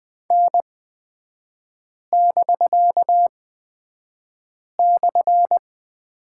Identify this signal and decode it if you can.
Morse code, heard as a keyed tone, about 20 wpm, spaced as for 5 wpm, 700 Hz. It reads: N<BK>/
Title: 250 Lesson 40e - BK - Three Characters 20wpm